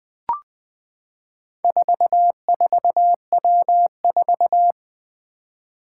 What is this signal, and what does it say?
Morse code, heard as a keyed tone, about 20 wpm, 700 Hz. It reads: 44W4
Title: Four Challenging Letter-Number Combinations 20wpm